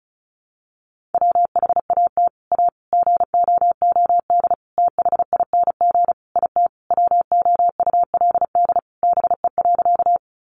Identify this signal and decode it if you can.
Morse code, heard as a keyed tone, about 34 wpm, 700 Hz. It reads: WHAT A GOOD THING IT WOULD BE.
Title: Sentences from Top 200 Words 35wpm